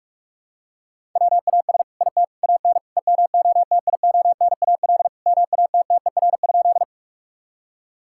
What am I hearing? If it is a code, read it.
Morse code, heard as a keyed tone, about 46 wpm, 700 Hz. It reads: WAS IT AN EMOTIONAL MATTER?
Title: Sentences from Top 1000 Words - Review Part 5 - 45wpm